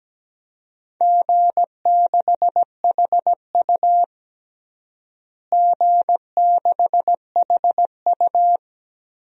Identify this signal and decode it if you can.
Morse code, heard as a keyed tone, about 17 wpm, 700 Hz. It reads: G6HU G6HU
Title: DXCC-100 Call Signs - 4 Characters - Repeated 2x - 17wpm